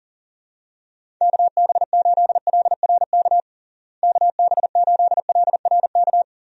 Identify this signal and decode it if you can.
Morse code, heard as a keyed tone, about 40 wpm, 700 Hz. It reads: KB8LRK KB8LRK
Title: USA Call Signs - 6 Characters - Repeated 3x - 40wpm